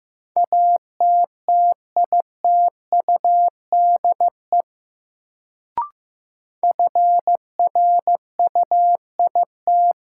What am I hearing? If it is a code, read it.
Morse code, heard as a keyed tone, about 15 wpm, 700 Hz. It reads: ATTITUDE FRUIT
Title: Sentences from Top 1300 Words - New Word Drill - 15wpm